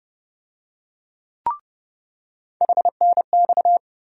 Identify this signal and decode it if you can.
Morse code, heard as a keyed tone, about 30 wpm, 700 Hz. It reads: HNX
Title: USA Call Sign Postfixes - 3 Letters 30wpm